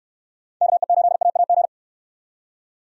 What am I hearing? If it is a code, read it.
Morse code, heard as a keyed tone, about 65 wpm, 700 Hz. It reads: KE8AAG